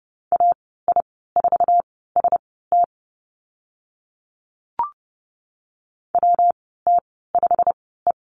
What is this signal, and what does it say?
Morse code, heard as a keyed tone, about 30 wpm, spaced as for 17 wpm, 700 Hz. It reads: AI4ST WT5E
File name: Lesson 19g - Letter w - Call Signs 30wpm